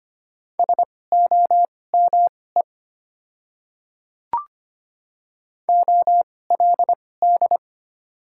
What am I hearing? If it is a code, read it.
Morse code, heard as a keyed tone, about 25 wpm, spaced as for 18 wpm, 700 Hz. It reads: SOME OLD
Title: 2x Character Spacing - Top 200 Words - 25wpm